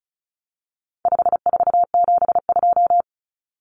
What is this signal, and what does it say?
Morse code, heard as a keyed tone, about 34 wpm, 700 Hz. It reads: 5472
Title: Common Cut Numbers - 4-Digit 35wpm